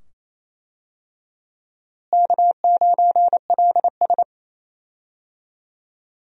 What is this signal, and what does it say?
Morse code, heard as a keyed tone, about 27 wpm, 700 Hz. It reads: K9LS
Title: USA Call Signs - 4 Characters - Repeated 2x - 28wpm